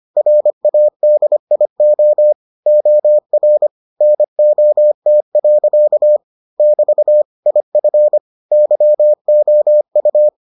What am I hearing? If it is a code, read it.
Morse code, heard as a keyed tone, about 25 wpm, 600 Hz. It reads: RADIO OR NOT. <BT> IF YOU